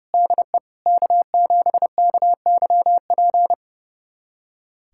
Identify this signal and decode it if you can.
Morse code, heard as a keyed tone, about 30 wpm, 700 Hz. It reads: DE K7KYP
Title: USA Callsign DE Callsign - 5 Characters - 30wpm